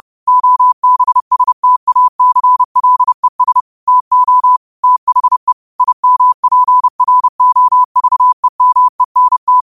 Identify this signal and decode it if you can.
Morse code, heard as a keyed tone, about 30 wpm, 1000 Hz. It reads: OBSTACLES TO THE IMPROVEMENT
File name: qotdfiles / 30wpm